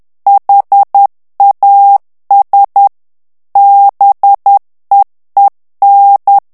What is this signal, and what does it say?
Morse code, heard as a keyed tone, about 10 wpm, 800 Hz. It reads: HAS BEEN
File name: ANGLAIS texte clair 3 vitesse 10 mots minute